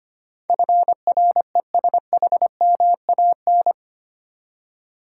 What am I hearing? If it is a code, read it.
Morse code, heard as a keyed tone, about 25 wpm, 700 Hz. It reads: FRESHMAN
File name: Eight Letter Words - 25wpm